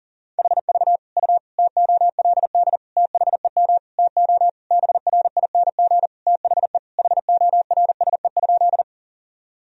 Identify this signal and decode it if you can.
Morse code, heard as a keyed tone, about 40 wpm, 700 Hz. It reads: HV U TOLD THEM TO BRING THE HORSE?
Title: Sentences from Top 500 Words and Abbreviations 40wpm